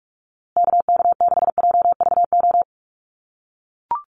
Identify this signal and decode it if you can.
Morse code, heard as a keyed tone, about 44 wpm, 700 Hz. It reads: KK6JVO
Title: USA Call Signs Repeated 3x 45wpm